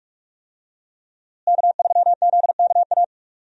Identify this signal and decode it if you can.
Morse code, heard as a keyed tone, about 46 wpm, 700 Hz. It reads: K3ZKA
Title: USA Call Signs - 5 Characters - Repeated 2x - 45wpm